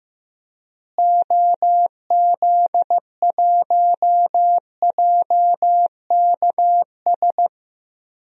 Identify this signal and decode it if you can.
Morse code, heard as a keyed tone, about 15 wpm, 700 Hz. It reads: OZ1JKS
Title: DXCC-100 Call Signs - 6 Characters - Repeated 2x - 15wpm